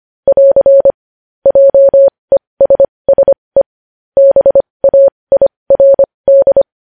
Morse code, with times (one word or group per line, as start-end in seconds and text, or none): 0.3-0.9 <AR>
1.5-3.6 JESSE
4.2-6.6 BAIRD